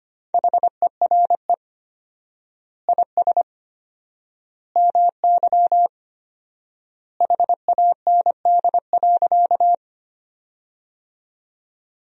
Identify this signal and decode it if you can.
Morse code, heard as a keyed tone, about 25 wpm, 700 Hz. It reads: HERE IS MY HAND.
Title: 4x Word-Spacing - Sentences from Top 200 Words - 25wpm